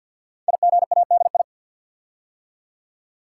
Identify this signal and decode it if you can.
Morse code, heard as a keyed tone, about 52 wpm, 700 Hz. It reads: IGADI
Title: Code Groups - 5 Letters - 50wpm